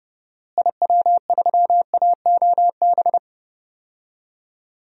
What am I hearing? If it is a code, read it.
Morse code, heard as a keyed tone, about 30 wpm, 700 Hz. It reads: IW3AOB
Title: DXCC-100 Call Signs - 6 Characters - Repeated 2x - 30wpm